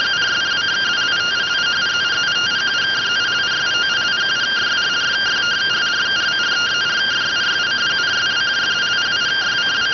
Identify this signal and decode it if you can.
Two-tone teleprinter signal, about 170 Hz shift.